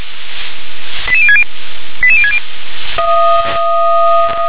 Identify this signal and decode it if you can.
Five-tone selective call (ZVEI), heard as quick runs of five tones, tones of about 70 ms. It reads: 88060 88060
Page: SDR/ZVEI